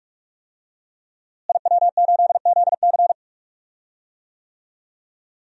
Morse code, heard as a keyed tone, about 46 wpm, 700 Hz. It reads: IW8ZC
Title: DXCC-100 Call Signs - 5 Characters - Repeated 3x - 45wpm